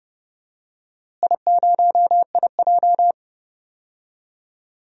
Morse code, heard as a keyed tone, about 31 wpm, 700 Hz. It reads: I0IJ
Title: DXCC-100 Call Signs - 4 Characters - Repeated 2x - 30wpm